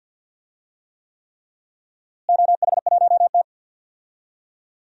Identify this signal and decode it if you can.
Morse code, heard as a keyed tone, about 52 wpm, 700 Hz. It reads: OH1T
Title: DXCC-100 Call Signs - 4 Characters - Repeated 3x - 50wpm